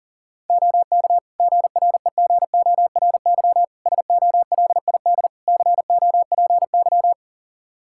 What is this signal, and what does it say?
Morse code, heard as a keyed tone, about 40 wpm, 700 Hz. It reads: OK GREGORY SOLID COPY
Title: Practice QSO - Exchange 3 - More Info - 40wpm